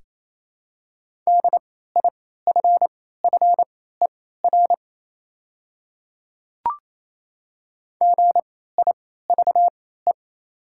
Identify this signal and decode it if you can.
Morse code, heard as a keyed tone, about 28 wpm, spaced as for 16 wpm, 700 Hz. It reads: DIFFER GIVE
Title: 3x Character Spacing - Top 300 Words - 28wpm